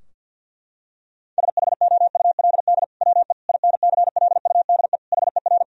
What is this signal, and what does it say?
Morse code, heard as a keyed tone, about 50 wpm, 700 Hz. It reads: SHOULD WE INCLUDE HER
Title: Sentences from Top 1000 Words - Review Part 3 - 50wpm